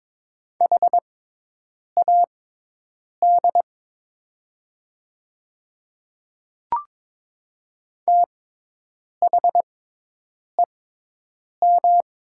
Morse code, heard as a keyed tone, about 22 wpm, spaced as for 8 wpm, 700 Hz. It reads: HAD THEM
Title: 6x Character Spacing - Top 100 Words - 22wpm